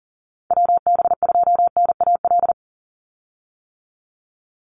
Morse code, heard as a keyed tone, about 38 wpm, 700 Hz. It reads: WB2NAL
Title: USA Call Signs 40wpm